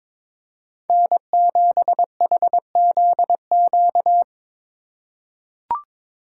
Morse code, heard as a keyed tone, about 22 wpm, 700 Hz. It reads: N7HZQ
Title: USA Call Signs - 4x Character Spacing - 22wpm